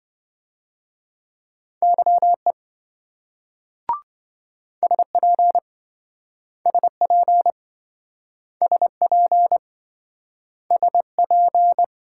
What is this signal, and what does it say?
Morse code, heard as a keyed tone, about 27 wpm, 700 Hz. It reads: YE SP SP SP SP